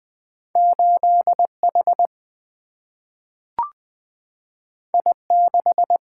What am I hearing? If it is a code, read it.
Morse code, heard as a keyed tone, about 20 wpm, 700 Hz. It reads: 8 H I 6
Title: Single-Letter Single-Number 20wpm